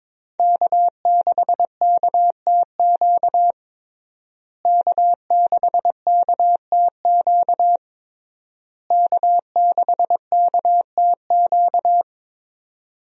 Morse code, heard as a keyed tone, about 22 wpm, 700 Hz. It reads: K6KTQ K6KTQ K6KTQ
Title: USA Call Signs - 5 Characters - Repeated 3x - 22wpm